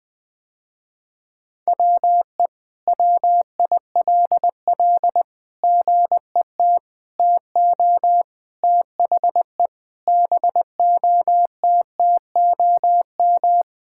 Morse code, heard as a keyed tone, about 20 wpm, 700 Hz. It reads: WE WILL GET TO THE BOTTOM
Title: Sentences with 6-Word Phrases - Spoken Beginning - 20wpm